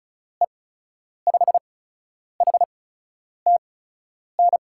Morse code, heard as a keyed tone, about 36 wpm, spaced as for 10 wpm, 700 Hz. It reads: E5HTN
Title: USA Call Signs - 6 Characters - 8x Character Spacing - 35wpm